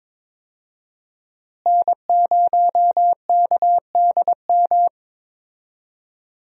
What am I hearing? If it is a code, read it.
Morse code, heard as a keyed tone, about 22 wpm, 700 Hz. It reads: N0KDM
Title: USA Call Signs - 5 Characters - Repeated 2x - 22wpm